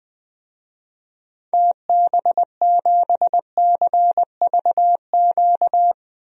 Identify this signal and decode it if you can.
Morse code, heard as a keyed tone, about 21 wpm, 700 Hz. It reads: TB7CVQ
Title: DXCC-100 Call Signs - 6 Characters - 20wpm